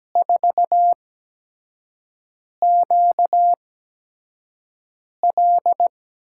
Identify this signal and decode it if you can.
Morse code, heard as a keyed tone, about 17 wpm, spaced as for 5 wpm, 700 Hz. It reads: 4QL
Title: USA Call Signs - 4 Characters - 8x Character Spacing - 17wpm